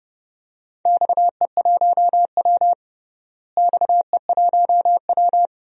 Morse code, heard as a keyed tone, about 31 wpm, 700 Hz. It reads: XE1W XE1W
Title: DXCC-100 Call Signs - 4 Characters - Repeated 2x - 30wpm